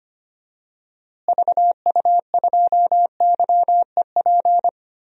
Morse code, heard as a keyed tone, about 25 wpm, 700 Hz. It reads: VU2YEP
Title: DXCC-100 Call Signs - 6 Characters - Repeated 2x - 25wpm